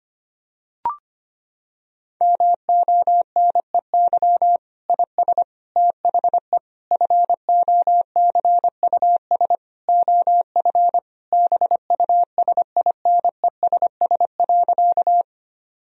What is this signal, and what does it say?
Morse code, heard as a keyed tone, about 25 wpm, 700 Hz. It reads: MONEY IS THE FOCUS OF BUSINESS.
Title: Sentences from Top 1000 Words - Review Part 4 - 25wpm